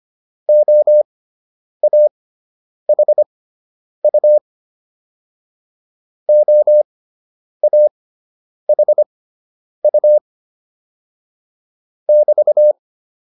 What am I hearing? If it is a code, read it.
Morse code, heard as a keyed tone, about 25 wpm, spaced as for 9 wpm, 600 Hz. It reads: OAHU OAHU <BT>